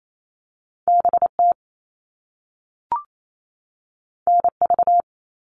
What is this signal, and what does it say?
Morse code, heard as a keyed tone, about 28 wpm, 700 Hz. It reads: BT NV
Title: All Cut Numbers - 2-Digit 28wpm